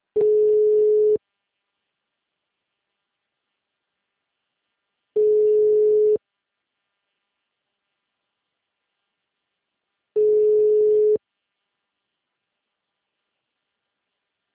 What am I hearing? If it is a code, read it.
Ringback tone (Europe, 425 Hz).
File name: out- -2041- - - .39260